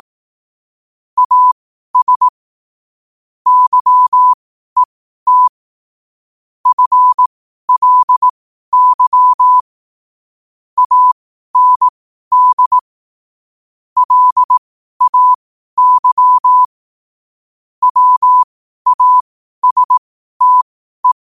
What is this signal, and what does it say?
Morse code, heard as a keyed tone, about 18 wpm, spaced as for 12 wpm, 1000 Hz. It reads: AS YET FLY AND LAY WASTE
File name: qotdfiles / 12wpm